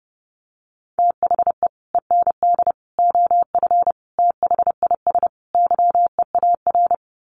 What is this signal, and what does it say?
Morse code, heard as a keyed tone, about 30 wpm, 700 Hz. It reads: THE END OF THIS YEAR
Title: Sets of 5 Words Encore 30wpm